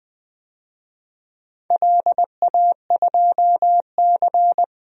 Morse code, heard as a keyed tone, about 20 wpm, 700 Hz. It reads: LA2C
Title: DXCC-100 Call Signs - 4 Characters - 20wpm